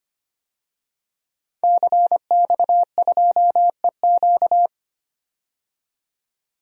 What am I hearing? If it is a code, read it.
Morse code, heard as a keyed tone, about 25 wpm, 700 Hz. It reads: CX2EQ
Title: DXCC-100 Call Signs - 5 Characters - 25wpm